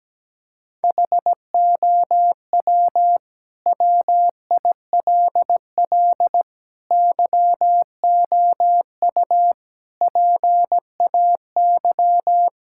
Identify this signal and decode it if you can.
Morse code, heard as a keyed tone, about 17 wpm, 700 Hz. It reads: HOW WILL YOU PAY
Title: Sentences from Top 1000 Words - Review Part 5 - 17wpm